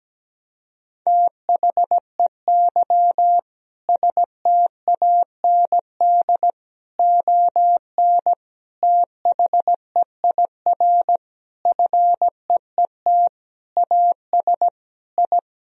Morse code, heard as a keyed tone, about 17 wpm, 700 Hz. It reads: THEY STAND ON THEIR FEET AS I
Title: Sentences from Top 1000 Words - Review Part 1 - 17wpm